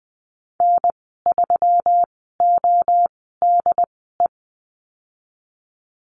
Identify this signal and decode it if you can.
Morse code, heard as a keyed tone, about 20 wpm, spaced as for 15 wpm, 700 Hz. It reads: N3ODE
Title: 147 Lesson 24g - Letter y - Call Signs 20wpm